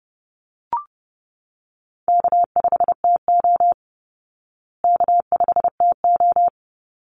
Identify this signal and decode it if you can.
Morse code, heard as a keyed tone, about 30 wpm, 700 Hz. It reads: K5TO K5TO
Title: Call Signs Repeated 3x - 30wpm